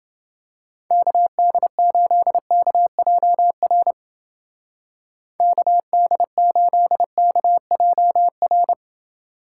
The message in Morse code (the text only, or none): KD8KJR KD8KJR